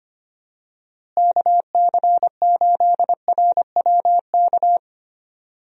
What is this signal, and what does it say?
Morse code, heard as a keyed tone, about 25 wpm, 700 Hz. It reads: KC8RWK
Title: USA Call Signs - 6 Characters - Repeated 2x - 25wpm